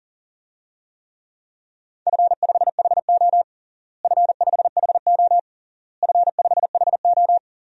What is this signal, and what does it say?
Morse code, heard as a keyed tone, about 40 wpm, 700 Hz. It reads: F5HO F5HO F5HO
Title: DXCC-100 Call Signs - 4 Characters - Repeated 3x - 40wpm